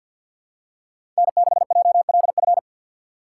Morse code, heard as a keyed tone, about 47 wpm, 690 Hz. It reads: N6JLF
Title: USA Call Signs - 5 Characters - 6x Character Spacing - 50wpm